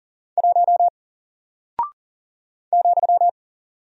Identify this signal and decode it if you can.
Morse code, heard as a keyed tone, about 41 wpm, 700 Hz. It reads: 1 ,
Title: Single Character (Letter:Number:Punctuation:Prosign) Alt 40wpm